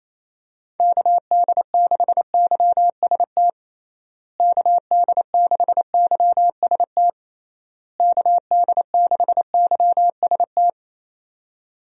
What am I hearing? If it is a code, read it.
Morse code, heard as a keyed tone, about 27 wpm, 700 Hz. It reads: KD6YST KD6YST KD6YST
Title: USA Call Signs - 6 Characters - Repeated 3x - 28wpm